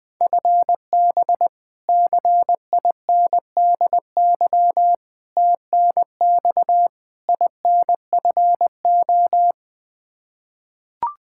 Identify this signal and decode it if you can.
Morse code, heard as a keyed tone, about 20 wpm, 700 Hz. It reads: FB CINDY TNX INFO
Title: Practice QSO - Exchange 3 - More Info - 20wpm